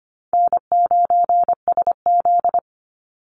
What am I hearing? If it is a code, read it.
Morse code, heard as a keyed tone, about 25 wpm, 700 Hz. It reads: N9SZ
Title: Easy Call Signs - 25wpm